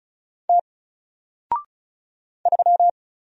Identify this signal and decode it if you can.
Morse code, heard as a keyed tone, about 36 wpm, 700 Hz. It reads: T 3
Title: Single Character (Letter:Number:Punctuation:Prosign) Alt 35wpm